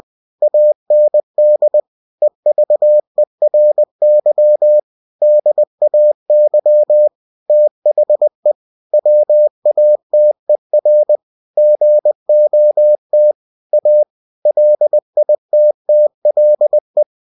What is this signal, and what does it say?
Morse code, heard as a keyed tone, about 20 wpm, 600 Hz. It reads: AND EVERY DAY THE WATER GOT A LITTLE
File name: cw-books / winnie-the-pooh / winnie-20wpm-page-0091